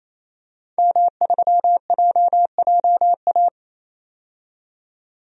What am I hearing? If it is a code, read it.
Morse code, heard as a keyed tone, about 28 wpm, 700 Hz. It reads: M3JJA
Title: DXCC-100 Call Signs - 5 Characters - 28wpm